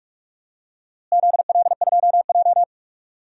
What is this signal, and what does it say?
Morse code, heard as a keyed tone, about 41 wpm, 700 Hz. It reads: ZL2J
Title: DXCC-100 Call Signs - 4 Characters - Repeated 3x - 45wpm